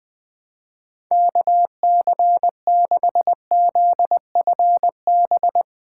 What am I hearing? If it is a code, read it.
Morse code, heard as a keyed tone, about 21 wpm, 700 Hz. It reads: KC6ZFB
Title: USA Callsign DE Callsign 20wpm